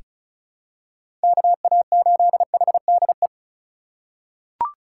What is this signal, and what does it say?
Morse code, heard as a keyed tone, about 34 wpm, 700 Hz. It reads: KA8HDE
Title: K1USN SST QSO Practice - 35wpm